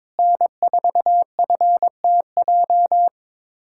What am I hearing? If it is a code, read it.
Morse code, heard as a keyed tone, about 22 wpm, 700 Hz. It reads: N4FTJ
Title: USA Callsign DE Callsign - 5 Characters - 22wpm